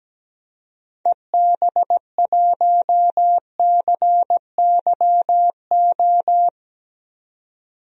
Morse code, heard as a keyed tone, about 17 wpm, 700 Hz. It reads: EB1CYO
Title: DXCC-100 Call Signs - 6 Characters - 17wpm